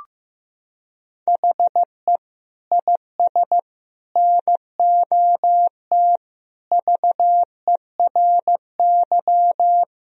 Morse code, heard as a keyed tone, about 15 wpm, 700 Hz. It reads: HE IS NOT VERY
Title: Sentences from Top 1000 Words - Review Part 1 - 15wpm